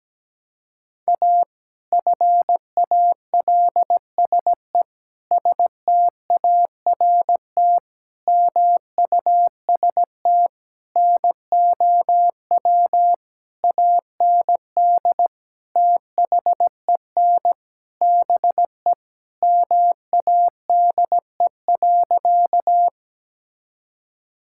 Morse code, heard as a keyed tone, about 17 wpm, 700 Hz. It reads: A FALSE START MUST NOW AND THEN BE MADE.
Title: Sentences from Top 1600 Words 17wpm